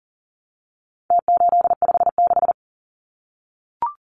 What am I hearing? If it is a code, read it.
Morse code, heard as a keyed tone, about 37 wpm, 700 Hz. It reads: T856
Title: Numbers - Contest Style - 4-Digit 40wpm